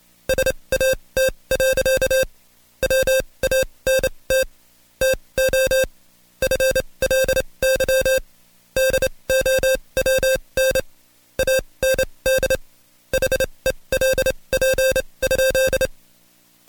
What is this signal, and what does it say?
Morse code, heard as a keyed tone, about 27 wpm, spaced as for 22 wpm, 530 Hz. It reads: SAT. WANT TO FLY DOWN AND HELP?
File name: Code Quick / AudioDownld2 / Seg137-Stepper10 21WPM